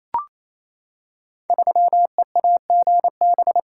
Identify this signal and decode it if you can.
Morse code, heard as a keyed tone, about 27 wpm, 700 Hz. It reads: 3EAGB
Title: Code Groups - 5 Letter:Numbers - 28wpm